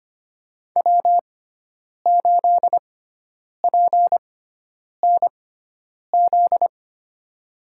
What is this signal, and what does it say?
Morse code, heard as a keyed tone, about 25 wpm, spaced as for 9 wpm, 700 Hz. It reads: W8PNZ
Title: USA Call Signs - 5 Characters - 6x Character Spacing - 25wpm